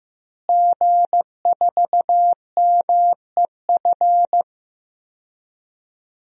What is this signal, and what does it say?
Morse code, heard as a keyed tone, about 15 wpm, 700 Hz. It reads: G4MEF
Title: DXCC-100 Call Signs - 5 Characters - Repeated 2x - 15wpm